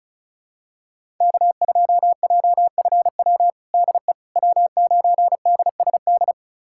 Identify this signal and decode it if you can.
Morse code, heard as a keyed tone, about 34 wpm, 700 Hz. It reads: K2JFW DE W9DSD
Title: USA Callsign DE Callsign - 5 Characters - 35wpm